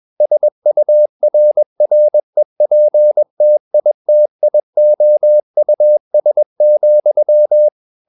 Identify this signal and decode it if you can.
Morse code, heard as a keyed tone, about 21 wpm, 600 Hz. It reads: SURREPTITIOUS,
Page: CW med Gnister